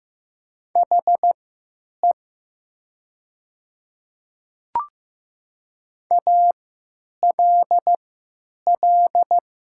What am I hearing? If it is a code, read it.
Morse code, heard as a keyed tone, about 15 wpm, spaced as for 9 wpm, 700 Hz. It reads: HE ALL
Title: 3x Character Spacing - Top 100 Words - 15wpm